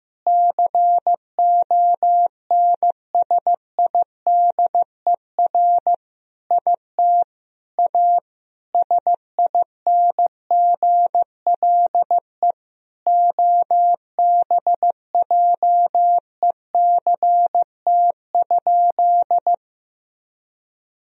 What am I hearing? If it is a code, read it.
Morse code, heard as a keyed tone, about 15 wpm, 700 Hz. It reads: CONSIDER IT A SINGLE OBJECT?
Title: Sentences from Top 1000 Words - Review Part 4 - 15wpm